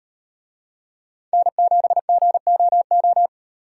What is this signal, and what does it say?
Morse code, heard as a keyed tone, about 37 wpm, 700 Hz. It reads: N7GOO